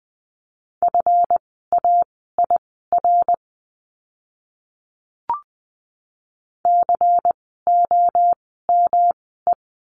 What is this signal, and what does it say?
Morse code, heard as a keyed tone, about 20 wpm, spaced as for 15 wpm, 700 Hz. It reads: FAIR COME
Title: 139 Lesson 23f - Letter f - Words 20wpm